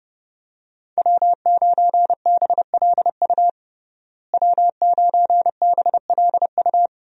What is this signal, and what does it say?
Morse code, heard as a keyed tone, about 30 wpm, 700 Hz. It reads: W9BLU W9BLU
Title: USA Call Signs - 5 Characters - Repeated 2x - 30wpm